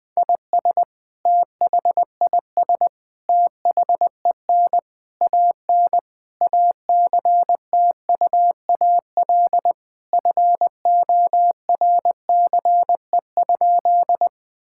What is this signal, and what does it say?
Morse code, heard as a keyed tone, about 20 wpm, 700 Hz. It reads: IS THIS THEN AN ACTUAL FORCE?
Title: Sentences from Top 1000 Words - Review Part 5 - 20wpm